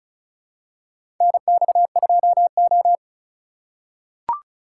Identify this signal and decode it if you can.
Morse code, heard as a keyed tone, about 34 wpm, 700 Hz. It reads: NX2O
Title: USA Call Signs - 4 Characters - 2x Character Spacing - 35wpm